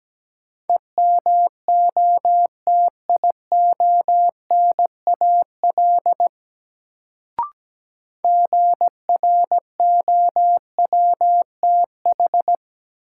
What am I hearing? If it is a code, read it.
Morse code, heard as a keyed tone, about 17 wpm, 700 Hz. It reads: EMOTIONAL GROWTH
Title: Sentences from Top 1000 Words - Review Part 5 - New Word Drill - 17wpm